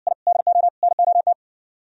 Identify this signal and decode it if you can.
Morse code, heard as a keyed tone, about 65 wpm, 700 Hz. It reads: I DO NOT